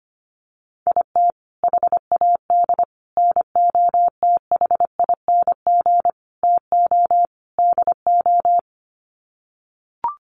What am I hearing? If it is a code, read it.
Morse code, heard as a keyed tone, about 24 wpm, 700 Hz. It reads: IT HAD NOTHING TO DO
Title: Sets of 5 Words Encore2 25wpm